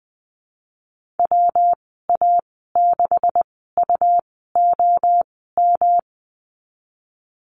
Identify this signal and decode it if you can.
Morse code, heard as a keyed tone, about 19 wpm, spaced as for 14 wpm, 700 Hz. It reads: WA6UOM
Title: 167 Lesson 27g - Number 7 - Call Signs 20wpm